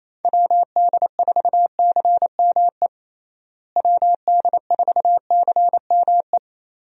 Morse code, heard as a keyed tone, about 28 wpm, 700 Hz. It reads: WD4CME WD4CME
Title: USA Call Signs - 6 Characters - Repeated 3x - 28wpm